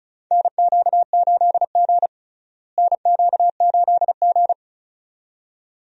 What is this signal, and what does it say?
Morse code, heard as a keyed tone, about 36 wpm, 700 Hz. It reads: NQ8G NQ8G
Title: USA Call Signs - 4 Characters - Repeated 2x - 35wpm